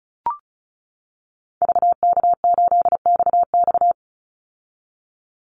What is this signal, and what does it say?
Morse code, heard as a keyed tone, about 34 wpm, 700 Hz. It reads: VK8XX
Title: Call Signs - 35wpm